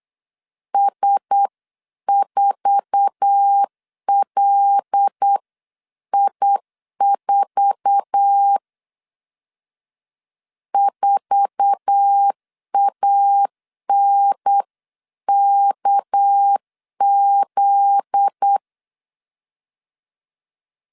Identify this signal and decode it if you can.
Morse code, heard as a keyed tone, about 8 wpm, 800 Hz. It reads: S4LI4 4ANKZ
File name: LEZIONI / 8 YJ45 / 8 YJ4 8wpm 3246